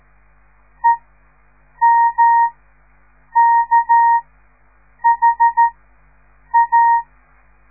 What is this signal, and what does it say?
Morse code, heard as a keyed tone, about 13 wpm, spaced as for 7 wpm, 930 Hz. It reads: EMKHA
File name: radiomorningstar / morse / les 6